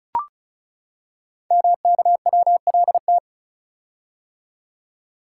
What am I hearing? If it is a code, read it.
Morse code, heard as a keyed tone, about 36 wpm, 700 Hz. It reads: MKWLT
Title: Code Groups - 5 Letter:Numbers - 35wpm